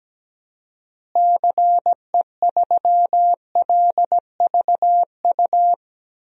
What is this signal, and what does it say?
Morse code, heard as a keyed tone, about 17 wpm, 700 Hz. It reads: CE3LVU
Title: DXCC-100 Call Signs - 6 Characters - Repeated 2x - 17wpm